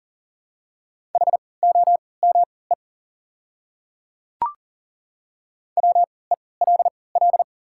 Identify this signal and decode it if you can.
Morse code, heard as a keyed tone, about 40 wpm, spaced as for 23 wpm, 700 Hz. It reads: HOME WELL
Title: 3x Character Spacing - Top 300 Words - 40wpm